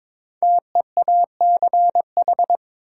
Morse code, heard as a keyed tone, about 22 wpm, 700 Hz. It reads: TEACH
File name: 5 Letter Words - First 3 Letters Given - 22wpm